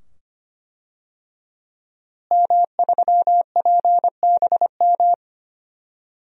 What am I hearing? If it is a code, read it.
Morse code, heard as a keyed tone, about 25 wpm, 700 Hz. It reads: M3PBM
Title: DXCC-100 Call Signs - 5 Characters - Repeated 2x - 25wpm